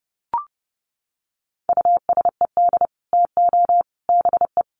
Morse code, heard as a keyed tone, about 30 wpm, 700 Hz. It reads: USED TO BE
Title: Sets of 3 Words 30wpm